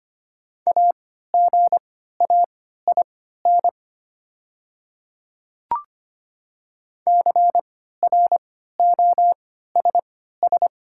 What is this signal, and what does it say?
Morse code, heard as a keyed tone, about 25 wpm, spaced as for 14 wpm, 700 Hz. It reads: AGAIN CROSS
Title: 3x Character Spacing - Top 300 Words - 25wpm